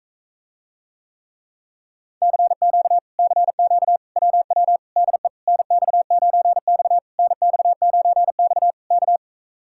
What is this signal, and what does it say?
Morse code, heard as a keyed tone, about 44 wpm, 700 Hz. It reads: CQ CQ WW DE NX9X NX9X K